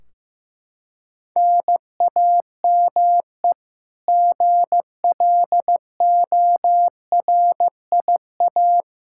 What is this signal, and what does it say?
Morse code, heard as a keyed tone, about 15 wpm, 700 Hz. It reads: NAME GLORIA
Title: K1USN SST QSO Practice 15wpm